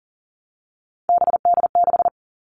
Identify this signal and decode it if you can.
Morse code, heard as a keyed tone, about 40 wpm, 700 Hz. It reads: BD6
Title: All Cut Numbers - 3-Digit 40wpm